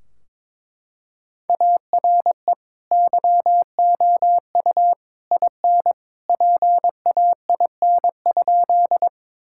Morse code, heard as a keyed tone, about 22 wpm, 700 Hz. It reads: ARE YOU IN PAIN?
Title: Sentences from Top 1600 Words 22wpm